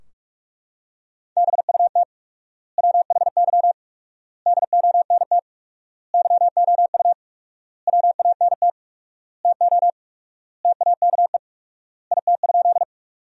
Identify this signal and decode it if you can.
Morse code, heard as a keyed tone, about 41 wpm, 700 Hz. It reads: BUT WHY DONT YOU WANT TO TAKE IT?
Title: 4x Word-Spacing - Sentences from Top 300 Words - 45wpm